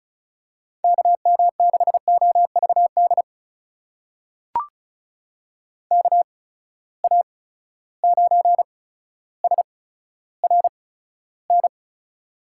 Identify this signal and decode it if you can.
Morse code, heard as a keyed tone, about 36 wpm, spaced as for 15 wpm, 700 Hz. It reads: KM6OVD KA9SRN
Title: USA Call Signs - 6 Characters - 8x Character Spacing - 35wpm